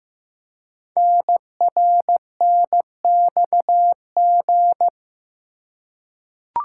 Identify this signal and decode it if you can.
Morse code, heard as a keyed tone, about 15 wpm, 700 Hz. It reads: NRNXG
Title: Code Groups - 5 Letters - 15wpm